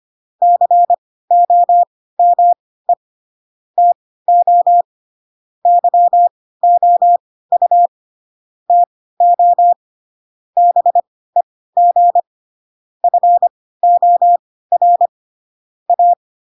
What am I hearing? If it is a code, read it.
Morse code, heard as a keyed tone, about 25 wpm, spaced as for 16 wpm, 700 Hz. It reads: COME TO YOU TO BEG FOR A
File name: Pinocchio / Pinocchio ch2 10 25 ik1hge